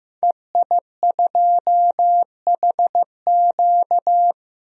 Morse code, heard as a keyed tone, about 15 wpm, 700 Hz. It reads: EI2HQ
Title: DXCC-100 Call Signs - 5 Characters - Repeated 3x - 15wpm